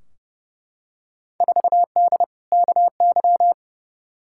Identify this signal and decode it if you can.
Morse code, heard as a keyed tone, about 30 wpm, 700 Hz. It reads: 4D KY
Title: Field Day Exchanges Encore - 30wpm